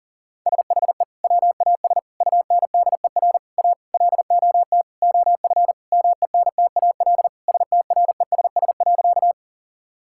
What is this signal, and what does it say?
Morse code, heard as a keyed tone, about 40 wpm, 700 Hz. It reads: SHE WAS UNDER A LOT OF MENTAL STRESS.
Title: Sentences from Top 1000 Words - Review Part 6 - 40wpm